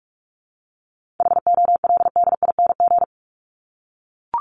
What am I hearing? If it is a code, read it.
Morse code, heard as a keyed tone, about 44 wpm, 700 Hz. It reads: HOLDING
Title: ING Suffix - Top 100 Words with Base Words 45wpm